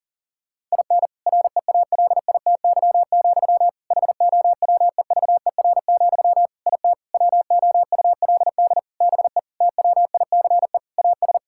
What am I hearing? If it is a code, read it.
Morse code, heard as a keyed tone, about 40 wpm, 700 Hz. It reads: IN REALITY, HOWEVER, IT WOULD BE TWICE AS
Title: Sentences from Top 1600 Words 40wpm